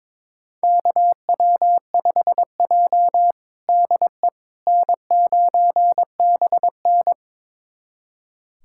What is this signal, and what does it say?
Morse code, heard as a keyed tone, about 22 wpm, 700 Hz. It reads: KW5J DE N9BN
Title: USA Callsign DE Callsign - 4 Characters - 22wpm